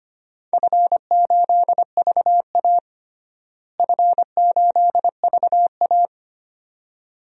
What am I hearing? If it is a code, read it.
Morse code, heard as a keyed tone, about 25 wpm, 700 Hz. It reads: F8VA F8VA
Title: DXCC-100 Call Signs - 4 Characters - Repeated 2x - 25wpm